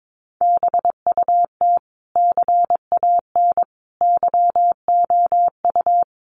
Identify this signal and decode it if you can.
Morse code, heard as a keyed tone, about 22 wpm, 700 Hz. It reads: BUT CAN YOU
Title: Sentences from Top 1400 Words 22wpm